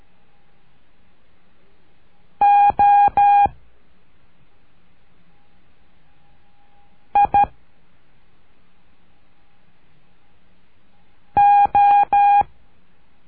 Morse code, heard as a keyed tone, about 12 wpm, 810 Hz. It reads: O I O
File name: cursos / CURSO DE TELEGRAFIA / Telegrafia - Trilha 01